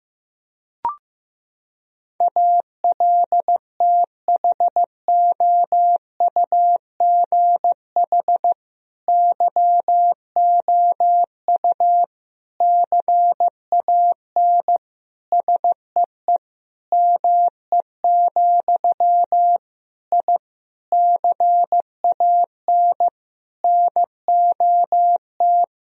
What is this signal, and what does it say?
Morse code, heard as a keyed tone, about 14 wpm, 700 Hz. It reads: ALTHOUGH YOU CAN SEE ME, I CAN NOT
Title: Sentences from Top 1000 Words - Review Part 3 - 15wpm